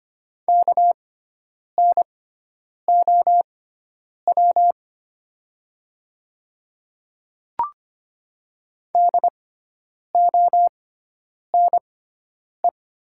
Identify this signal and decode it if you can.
Morse code, heard as a keyed tone, about 25 wpm, spaced as for 9 wpm, 700 Hz. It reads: KNOW DONE
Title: 6x Character Spacing - Top 500 Words - 25wpm